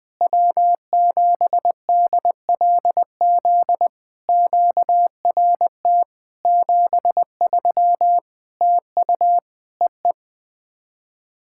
Morse code, heard as a keyed tone, about 20 wpm, 700 Hz. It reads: W7DLZ QRT 73 TU EE
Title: POTA QSO Practice - 20wpm